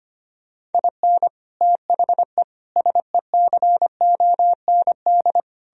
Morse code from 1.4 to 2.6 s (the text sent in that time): THE